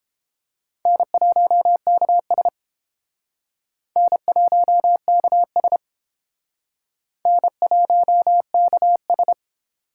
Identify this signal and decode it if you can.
Morse code, heard as a keyed tone, about 30 wpm, 700 Hz. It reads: N1KS N1KS N1KS